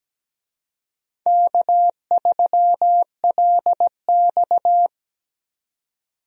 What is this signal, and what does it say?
Morse code, heard as a keyed tone, about 17 wpm, 700 Hz. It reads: K3LX
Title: USA Call Signs - 4 Characters - 17wpm